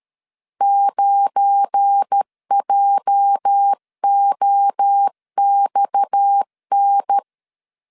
Morse code, heard as a keyed tone, about 13 wpm, 800 Hz. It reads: 9JOXN
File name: LEZIONI / 9 2 12wpm 2301